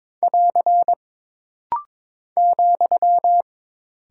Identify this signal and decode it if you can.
Morse code, heard as a keyed tone, about 22 wpm, 700 Hz. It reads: <AR> ,
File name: practice / 22wpm / Single Punctuation-Prosign - Rapid-Fire - 22wpm